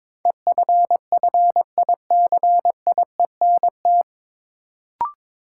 Morse code, heard as a keyed tone, about 22 wpm, 700 Hz. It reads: EFFICIENT
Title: Sentences from Top 1300 Words - New Word Drill - 22wpm